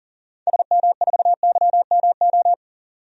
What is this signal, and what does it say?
Morse code, heard as a keyed tone, about 40 wpm, 700 Hz. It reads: SM4YMO
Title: DXCC-100 Call Signs - 6 Characters - 40wpm